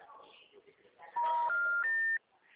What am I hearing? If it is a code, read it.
Special information tone, with speech.
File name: out- -2002- - - .9297